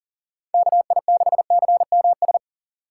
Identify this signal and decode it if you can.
Morse code, heard as a keyed tone, about 40 wpm, 700 Hz. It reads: KI6CMS
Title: USA Call Signs - 3x Character Spacing - 40wpm